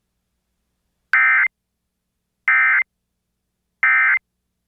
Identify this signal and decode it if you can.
Emergency Alert System (SAME) tones, heard as harsh buzzing bursts. It is heard from 1.1 to 4.2 s.